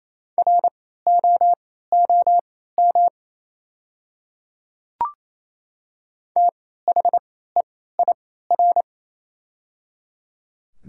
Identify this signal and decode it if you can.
Morse code, heard as a keyed tone, about 28 wpm, spaced as for 16 wpm, 700 Hz. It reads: ROOM THEIR
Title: 3x Character Spacing - Top 300 Words - 28wpm